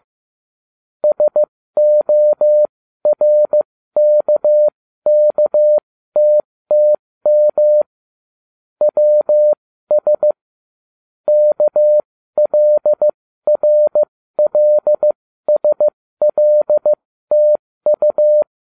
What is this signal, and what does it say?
Morse code, heard as a keyed tone, about 15 wpm, spaced as for 13 wpm, 600 Hz. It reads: SORKKTTM WS KLRLSLTU